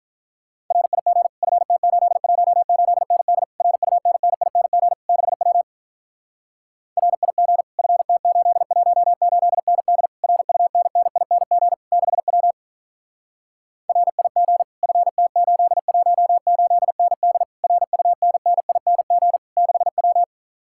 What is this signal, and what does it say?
Morse code, heard as a keyed tone, about 48 wpm, 700 Hz. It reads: RIG FT818ND RUNNING 6W RIG FT818ND RUNNING 6W RIG FT818ND RUNNING 6W